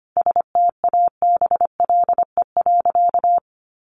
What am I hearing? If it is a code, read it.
Morse code, heard as a keyed tone, about 25 wpm, 700 Hz. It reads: STABLE.
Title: Sentences from Top 1500 Words 25wpm